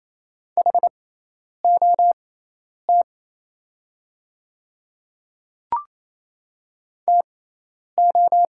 Morse code, heard as a keyed tone, about 27 wpm, spaced as for 10 wpm, 690 Hz. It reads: HOT TO
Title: 6x Character Spacing - Top 100 Words - 28wpm